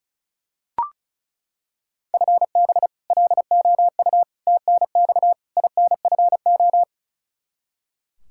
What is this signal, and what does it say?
Morse code, heard as a keyed tone, about 34 wpm, 700 Hz. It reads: FB LOU TNX INFO
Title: K1USN SST QSO Practice 35wpm